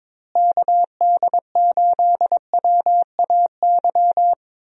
Morse code, heard as a keyed tone, about 21 wpm, 700 Hz. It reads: KD8WAY
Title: USA Call Signs - 6 Characters - Repeated 2x - 22wpm